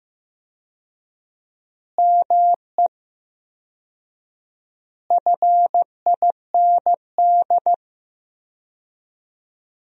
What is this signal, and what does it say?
Morse code, heard as a keyed tone, about 15 wpm, 700 Hz. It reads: ME FIND
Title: 4x Word-Spacing - Sentences from Top 200 Words - 15wpm